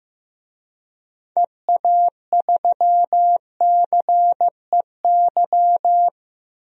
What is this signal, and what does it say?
Morse code, heard as a keyed tone, about 15 wpm, 700 Hz. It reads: EA3CEY
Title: DXCC-100 Call Signs - 6 Characters - Repeated 2x - 15wpm